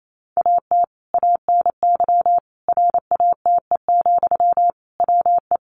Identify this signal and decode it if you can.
Morse code, heard as a keyed tone, about 28 wpm, 700 Hz. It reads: AT ANY RATE, WE
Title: Sentences from Top 700 Words 28wpm